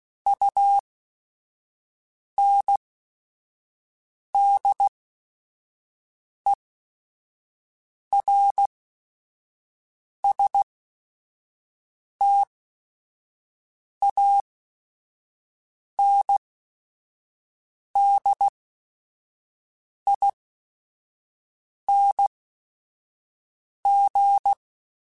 Morse code, heard as a keyed tone, about 16 wpm, spaced as for 5 wpm, 800 Hz. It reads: UNDERSTANDING
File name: Technical Subjects / Technical Subjects-mp3s / Sw Pwr Sup @5